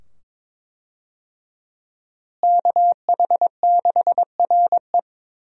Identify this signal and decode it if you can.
Morse code, heard as a keyed tone, about 22 wpm, 700 Hz. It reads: KH6RE
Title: USA Call Signs - 5 Characters - Repeated 2x - 22wpm